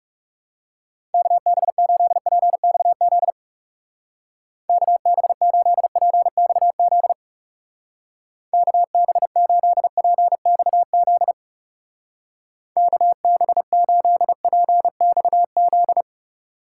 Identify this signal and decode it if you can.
Morse code, heard as a keyed tone, about 37 wpm, 700 Hz. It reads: KB8PXZ KB8PXZ KB8PXZ KB8PXZ
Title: Speed Racing - USA 2x3 Call Signs 30wpm